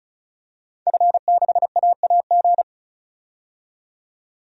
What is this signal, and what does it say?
Morse code, heard as a keyed tone, about 35 wpm, 700 Hz. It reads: F6AAG
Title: DXCC-100 Call Signs - 5 Characters - 35wpm